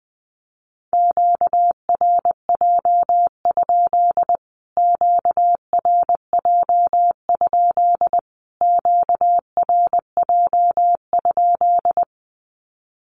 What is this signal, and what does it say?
Morse code, heard as a keyed tone, about 20 wpm, 700 Hz. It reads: QRJ? QRJ? QRJ?
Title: Q-Codes Repeated 3x 20wpm